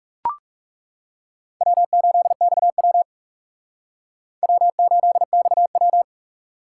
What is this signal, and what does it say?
Morse code, heard as a keyed tone, about 37 wpm, 700 Hz. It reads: W8XW W8XW
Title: Speed Racing - USA 1x2 Call Signs 30wpm